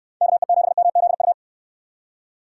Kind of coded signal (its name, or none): Morse code